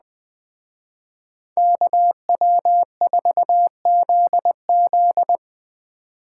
Morse code, heard as a keyed tone, about 20 wpm, 700 Hz. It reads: KW4ZZ
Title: Speed Racing - USA 2x2 Call Signs 17wpm